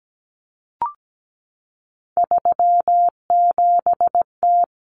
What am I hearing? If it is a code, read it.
Morse code, heard as a keyed tone, about 17 wpm, 690 Hz. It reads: 37T
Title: Common Cut Numbers - 3-Digit 17wpm